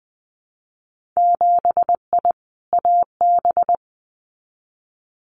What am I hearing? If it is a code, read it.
Morse code, heard as a keyed tone, about 20 wpm, 700 Hz. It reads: 7I AB
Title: Winter Field Day Exchanges Encore 20wpm